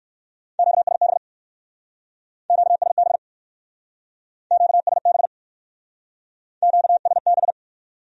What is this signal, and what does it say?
Morse code, heard as a keyed tone, about 60 wpm, 700 Hz. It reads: QSB QSB QSB QSB